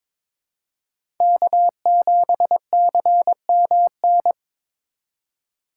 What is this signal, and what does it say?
Morse code, heard as a keyed tone, about 26 wpm, 700 Hz. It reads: K7CMN
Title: Speed Racing - USA 1x3 Call Signs 22wpm